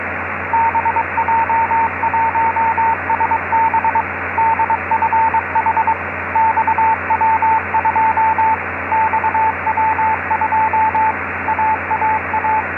Morse code, heard as a keyed tone, about 22 wpm, 880 Hz, with radio noise. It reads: BJ1SB DFH XW2 XW2 AAA
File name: xw-2a cw beacon 145mhz662 16h04 utc